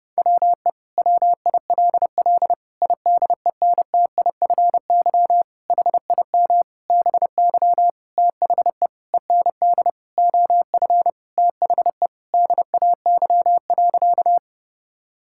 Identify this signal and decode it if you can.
Morse code, heard as a keyed tone, about 30 wpm, 700 Hz. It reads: WE WILL IDENTIFY HIM BY THE END OF THE DAY.
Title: Sentences from Top 1000 Words - Review Part 6 - 30wpm